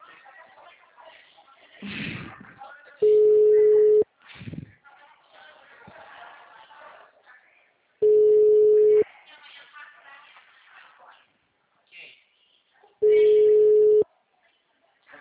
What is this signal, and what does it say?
Ringback tone (Europe, 425 Hz), with speech.